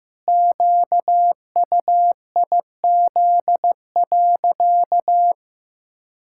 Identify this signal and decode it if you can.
Morse code, heard as a keyed tone, about 15 wpm, 700 Hz. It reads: QUIZ.
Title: Pangram Challenge v2 - 15wpm